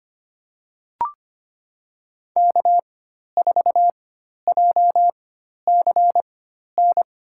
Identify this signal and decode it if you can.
Morse code, heard as a keyed tone, about 25 wpm, spaced as for 12 wpm, 700 Hz. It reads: K4JCN
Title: USA Call Signs - 5 Characters - 4x Character Spacing - 25wpm